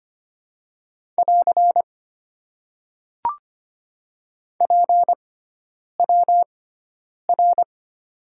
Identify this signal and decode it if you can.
Morse code, heard as a keyed tone, about 25 wpm, spaced as for 9 wpm, 700 Hz. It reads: <AR> PWR
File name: Top 90 QSO Elements - 6x Character Spacing - 25wpm